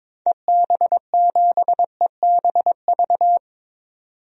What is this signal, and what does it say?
Morse code, heard as a keyed tone, about 22 wpm, 700 Hz. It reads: EB7EBV
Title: DXCC-100 Call Signs - 6 Characters - 22wpm